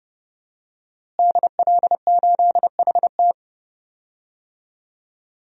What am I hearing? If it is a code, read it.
Morse code, heard as a keyed tone, about 30 wpm, 700 Hz. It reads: DL8HT
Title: DXCC-100 Call Signs - 5 Characters - Repeated 3x - 30wpm